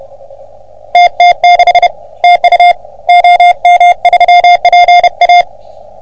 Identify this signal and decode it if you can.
Morse code, heard as a keyed tone, about 30 wpm, 720 Hz. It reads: TT6 TU OM3PA
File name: OM3PA PILUP CUC 388